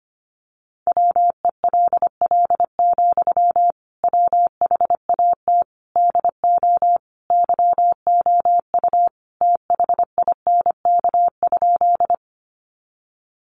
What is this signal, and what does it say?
Morse code, heard as a keyed tone, about 25 wpm, 700 Hz. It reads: WELL, WHAT DO YOU THINK?
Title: Sentences from Top 200 Words 25wpm